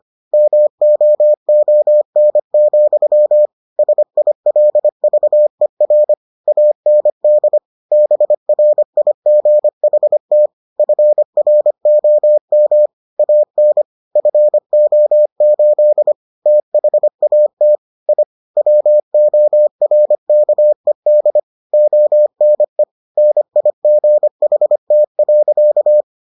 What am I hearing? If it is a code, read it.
Morse code, heard as a keyed tone, about 25 wpm, 600 Hz. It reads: MOON, SILVER AND BRIGHT FROM AN FO8 THAT I WORKED ONE NIGHT.